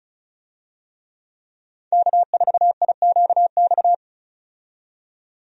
Morse code, heard as a keyed tone, about 36 wpm, 700 Hz. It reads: K4IQX
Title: USA Call Signs - 5 Characters - 35wpm